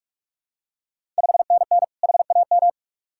Morse code, heard as a keyed tone, about 46 wpm, 700 Hz. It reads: 5NN HAM
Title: Ontario QSO Party Exchange 45wpm